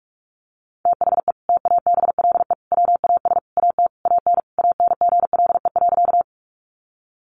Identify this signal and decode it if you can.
Morse code, heard as a keyed tone, about 44 wpm, 700 Hz. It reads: THE TABLE WAS AT AN ANGLE.
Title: Sentences from Top 1500 Words 45wpm